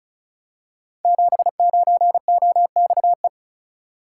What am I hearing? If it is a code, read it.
Morse code, heard as a keyed tone, about 36 wpm, 690 Hz. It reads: 79OXE
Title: Code Groups - 5 Letter:Numbers - 35wpm